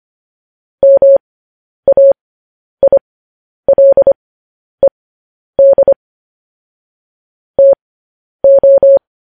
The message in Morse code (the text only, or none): MAILED TO